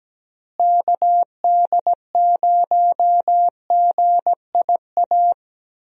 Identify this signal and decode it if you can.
Morse code, heard as a keyed tone, about 17 wpm, 700 Hz. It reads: KD0GIA
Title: USA Call Signs - 6 Characters - Repeated 3x - 17wpm